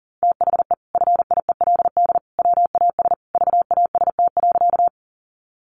Morse code, heard as a keyed tone, about 40 wpm, 700 Hz. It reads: THE FIELD WAS VAST.
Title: Sentences from Top 1300 Words 40wpm